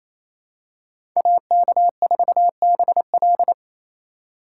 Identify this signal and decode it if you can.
Morse code, heard as a keyed tone, about 28 wpm, 700 Hz. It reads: AK4BL
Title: USA Call Signs - 5 Characters - 2x Character Spacing - 28wpm